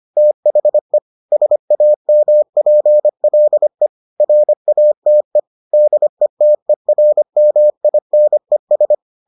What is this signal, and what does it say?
Morse code, heard as a keyed tone, about 25 wpm, 600 Hz. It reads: THE SAMPLE RATE DETERMINES